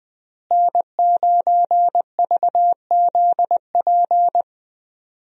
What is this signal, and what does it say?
Morse code, heard as a keyed tone, about 20 wpm, 700 Hz. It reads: N9VZP
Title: USA Call Signs - 5 Characters - Repeated 2x - 20wpm